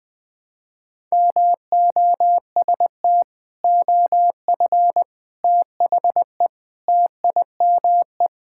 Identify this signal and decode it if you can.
Morse code, heard as a keyed tone, about 20 wpm, 700 Hz. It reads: MOST OF THE TIME
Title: Sentences with 4-Word Phrases - 20wpm